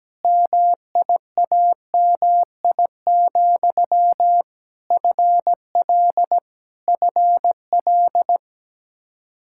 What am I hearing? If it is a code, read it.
Morse code, heard as a keyed tone, about 17 wpm, 700 Hz. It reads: MIAMI, FL FL
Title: K1USN SST QSO Practice 17wpm